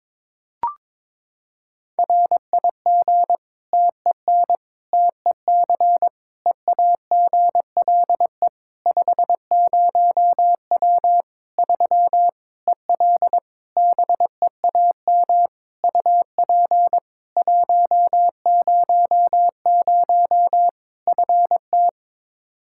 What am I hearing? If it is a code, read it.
Morse code, heard as a keyed tone, about 22 wpm, 700 Hz. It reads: RIG TEN TEC EAGLE 50W 3 EL BEAM UP 100 FT
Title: Practice QSO - Exchange 2 - Info - 22wpm